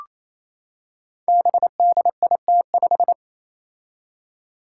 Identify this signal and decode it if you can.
Morse code, heard as a keyed tone, about 27 wpm, 700 Hz. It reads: BDIT5
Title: Five Challenging Letter-Number Combinations 28wpm